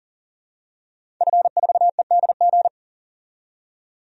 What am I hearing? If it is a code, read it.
Morse code, heard as a keyed tone, about 40 wpm, 700 Hz. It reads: F4EDG
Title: DXCC-100 Call Signs - 5 Characters - 40wpm